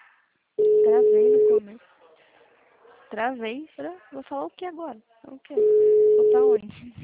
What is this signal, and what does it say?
Ringback tone (Europe, 425 Hz), with speech.